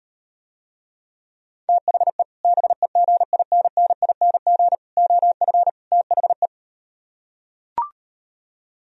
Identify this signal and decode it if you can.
Morse code, heard as a keyed tone, about 39 wpm, 700 Hz. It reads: THE BEGINNING OF THE